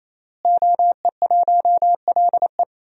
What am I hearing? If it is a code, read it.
Morse code, heard as a keyed tone, about 27 wpm, 700 Hz. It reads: OE1LE
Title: DXCC-100 Call Signs - 5 Characters - Repeated 3x - 28wpm